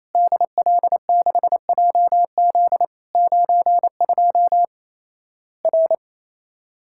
Morse code, heard as a keyed tone, about 28 wpm, 700 Hz. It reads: DL6JZ 92 R
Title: Worked All Europe DX Contest - QTC Exchange - 28wpm